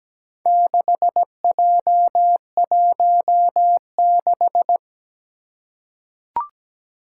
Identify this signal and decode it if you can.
Morse code, heard as a keyed tone, about 17 wpm, 700 Hz. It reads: 6J16
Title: Four Challenging Letter-Number Combinations 17wpm